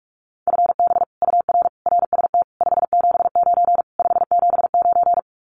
Morse code, heard as a keyed tone, about 44 wpm, 700 Hz. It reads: FB UR RST 579 579
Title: Common QSO Phrases Encore 45wpm